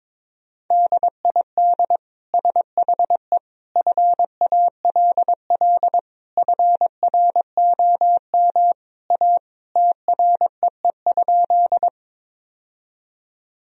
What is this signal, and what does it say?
Morse code, heard as a keyed tone, about 22 wpm, 700 Hz. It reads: DID SHE FALL FROM A TREE?
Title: Sentences from Top 1000 Words - Review Part 2 - 22wpm